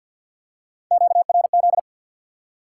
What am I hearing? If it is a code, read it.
Morse code, heard as a keyed tone, about 53 wpm, 700 Hz. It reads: QRZ
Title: Top 180 QSO Elements 50wpm